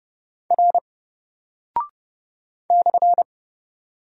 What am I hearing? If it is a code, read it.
Morse code, heard as a keyed tone, about 30 wpm, 700 Hz. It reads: R /
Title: Single Character (Letter:Number:Punctuation:Prosign) Alt 30wpm